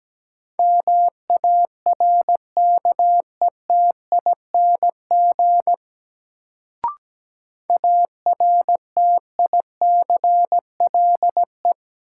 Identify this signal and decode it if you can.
Morse code, heard as a keyed tone, about 17 wpm, 700 Hz. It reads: MARKETING ARTICLE
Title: Sentences from Top 1000 Words - Review Part 5 - New Word Drill - 17wpm